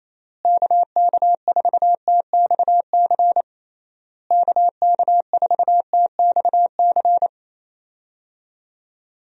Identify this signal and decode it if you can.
Morse code, heard as a keyed tone, about 28 wpm, 700 Hz. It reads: KK4TXC KK4TXC
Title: USA Call Signs - 6 Characters - Repeated 3x - 28wpm